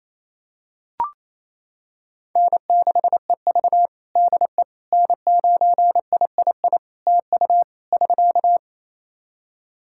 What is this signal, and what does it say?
Morse code, heard as a keyed tone, about 28 wpm, 700 Hz. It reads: N6EV DE N9III TU <SK>
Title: Practice QSO - Exchange 4 - Ending - 28wpm